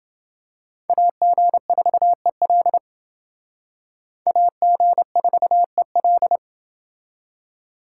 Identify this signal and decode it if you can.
Morse code, heard as a keyed tone, about 29 wpm, 700 Hz. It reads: AG4EL AG4EL